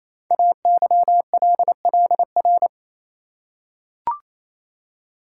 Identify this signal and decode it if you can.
Morse code, heard as a keyed tone, about 27 wpm, 700 Hz. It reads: AYLLR
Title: Code Groups - 5 Letters - 28wpm